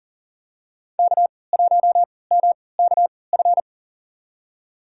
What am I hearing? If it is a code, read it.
Morse code, heard as a keyed tone, about 40 wpm, spaced as for 23 wpm, 700 Hz. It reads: K1MKF
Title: USA Call Signs - 5 Characters - 3x Character Spacing - 40wpm